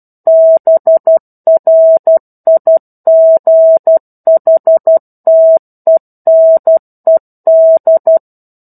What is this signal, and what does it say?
Morse code, heard as a keyed tone, about 12 wpm, 650 Hz. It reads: BRIGHTENED
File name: 01-Chapter-01 12 WPM